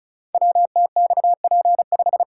Morse code, heard as a keyed tone, about 36 wpm, 700 Hz. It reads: WTXP5
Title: Code Groups - 5 Letter:Numbers - 35wpm